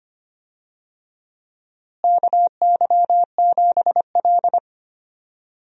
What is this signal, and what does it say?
Morse code, heard as a keyed tone, about 25 wpm, 700 Hz. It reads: KY7L
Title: USA Call Signs - 4 Characters - Repeated 2x - 25wpm